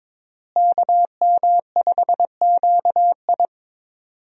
Morse code, heard as a keyed tone, about 22 wpm, 700 Hz. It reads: KM5QI
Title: USA Call Signs - 5 Characters - Repeated 3x - 22wpm